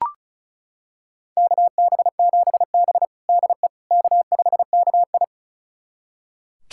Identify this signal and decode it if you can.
Morse code, heard as a keyed tone, about 36 wpm, 700 Hz. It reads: KB7B DE K5KI
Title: USA Callsign DE Callsign - 4 Characters - 35wpm